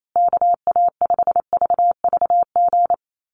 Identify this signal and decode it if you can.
Morse code, heard as a keyed tone, about 28 wpm, 700 Hz. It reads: KA5VVG
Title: USA Call Signs Repeated 3x 28wpm